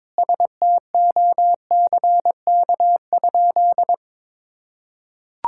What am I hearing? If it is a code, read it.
Morse code, heard as a keyed tone, about 22 wpm, 700 Hz. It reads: STOCK?
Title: Sentences from Top 1000 Words - Review Part 5 - 22wpm